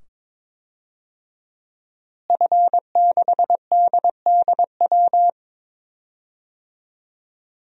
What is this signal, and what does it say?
Morse code, heard as a keyed tone, about 22 wpm, 700 Hz. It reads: F6DDW
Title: DXCC-100 Call Signs - 5 Characters - Repeated 2x - 22wpm